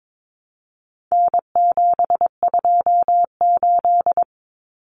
Morse code, heard as a keyed tone, about 22 wpm, 700 Hz. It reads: N728
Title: Common Cut Numbers - 4-Digit 22wpm